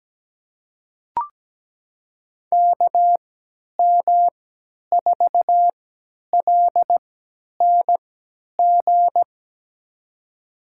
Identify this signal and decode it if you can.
Morse code, heard as a keyed tone, about 17 wpm, spaced as for 10 wpm, 700 Hz. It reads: KM4LNG
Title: USA Call Signs - 6 Characters - 3x Character Spacing - 17wpm